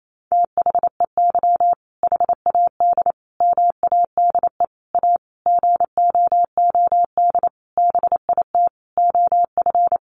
Morse code, heard as a keyed tone, about 28 wpm, 700 Hz. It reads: THEY HAD MADE A GOOD BIT OF
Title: Sentences from Top 900 Words 28wpm